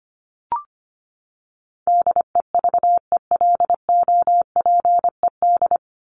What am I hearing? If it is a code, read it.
Morse code, heard as a keyed tone, about 25 wpm, 700 Hz. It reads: DEVELOPED
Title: ED Suffix - Top 100 Words with Base Words 25wpm